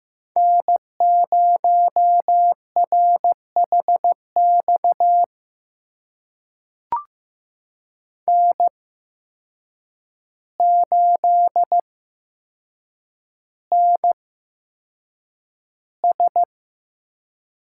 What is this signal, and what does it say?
Morse code, heard as a keyed tone, about 15 wpm, spaced as for 7 wpm, 700 Hz. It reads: N0RHX N8NS
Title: USA Call Signs - 5 Characters - 8x Character Spacing - 15wpm